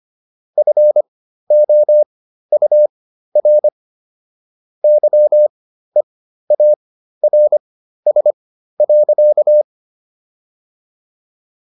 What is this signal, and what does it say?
Morse code, heard as a keyed tone, about 25 wpm, spaced as for 13 wpm, 600 Hz. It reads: FOUR YEARS.